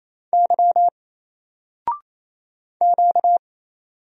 Morse code, heard as a keyed tone, about 27 wpm, 700 Hz. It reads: Y Q
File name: Single-Letter-Number-28wpm